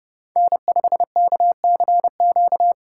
Morse code, heard as a keyed tone, about 30 wpm, 700 Hz. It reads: N5KCQ
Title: USA Call Signs - 3x Character Spacing - 30wpm